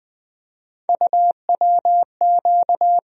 Morse code, heard as a keyed tone, about 20 wpm, 700 Hz. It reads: UWQ
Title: USA Call Sign Postfixes - 3 Letters 20wpm